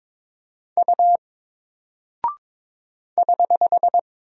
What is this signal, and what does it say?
Morse code, heard as a keyed tone, about 22 wpm, 700 Hz. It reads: U <HH>
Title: Single Character (Letter:Number:Punctuation:Prosign) Alt 22wpm